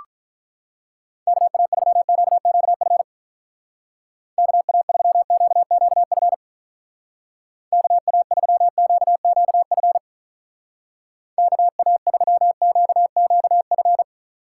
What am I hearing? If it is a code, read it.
Morse code, heard as a keyed tone, about 44 wpm, 700 Hz. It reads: KA3QQF KA3QQF KA3QQF KA3QQF